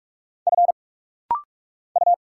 Morse code, heard as a keyed tone, about 46 wpm, 700 Hz. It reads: F U
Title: Single Letters - Rapid-Fire 45wpm